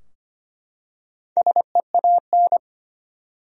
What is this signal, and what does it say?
Morse code, heard as a keyed tone, about 25 wpm, 700 Hz. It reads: SEAN
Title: USA Common Names - Max 4 Letters - 25wpm